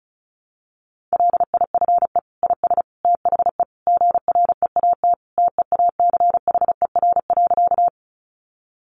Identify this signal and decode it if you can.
Morse code, heard as a keyed tone, about 34 wpm, 700 Hz. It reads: LIFE IS THE GREAT TEACHER.
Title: Sentences from Top 900 Words 35wpm